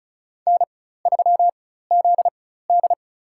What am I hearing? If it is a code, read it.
Morse code, heard as a keyed tone, about 34 wpm, spaced as for 16 wpm, 700 Hz. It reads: N3ZD
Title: USA Call Signs - 4 Characters - 4x Character Spacing - 35wpm